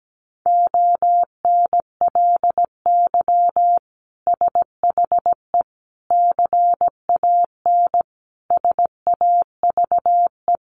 Morse code, heard as a keyed tone, about 17 wpm, 700 Hz. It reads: ONLY SHE CAN SAVE
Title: Sentences from Top 1100 Words 17wpm